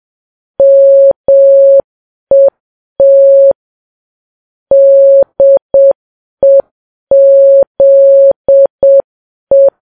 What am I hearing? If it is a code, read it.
Morse code, heard as a keyed tone, about 7 wpm, 550 Hz. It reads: MET DEZE
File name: Files Web tls / Morse on9fv / morse7kps